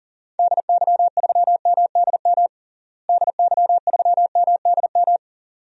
Morse code, heard as a keyed tone, about 40 wpm, 700 Hz. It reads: DY3MDM DY3MDM
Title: DXCC-100 Call Signs - 6 Characters - Repeated 2x - 40wpm